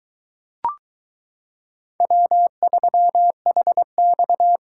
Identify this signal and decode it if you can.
Morse code, heard as a keyed tone, about 22 wpm, 700 Hz. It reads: W3HX